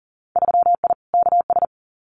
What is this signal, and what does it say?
Morse code, heard as a keyed tone, about 40 wpm, 700 Hz. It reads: 3I KS
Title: Winter Field Day Exchanges Encore 40wpm